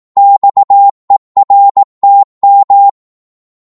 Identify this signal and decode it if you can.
Morse code, heard as a keyed tone, about 18 wpm, 800 Hz. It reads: XERTM
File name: Morse / ltr-18wpm